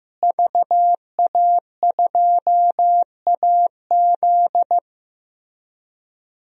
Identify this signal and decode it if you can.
Morse code, heard as a keyed tone, about 16 wpm, 700 Hz. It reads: VA2AZ
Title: DXCC-100 Call Signs - 5 Characters - Repeated 3x - 15wpm